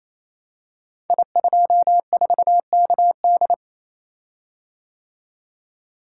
Morse code, heard as a keyed tone, about 28 wpm, 690 Hz. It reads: I24KD
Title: Code Groups - 5 Letter:Numbers - 28wpm